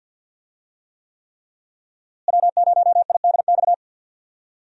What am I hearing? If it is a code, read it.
Morse code, heard as a keyed tone, about 52 wpm, 700 Hz. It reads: W0IDX
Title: USA Call Signs - 5 Characters - Repeated 3x - 50wpm